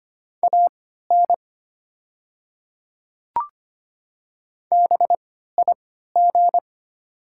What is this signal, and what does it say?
Morse code, heard as a keyed tone, about 25 wpm, spaced as for 14 wpm, 700 Hz. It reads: AN BIG
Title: 3x Character Spacing - Top 200 Words - 25wpm